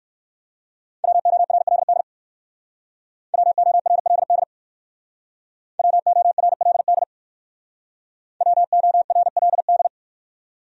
Morse code, heard as a keyed tone, about 58 wpm, 700 Hz. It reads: WORLD WORLD WORLD WORLD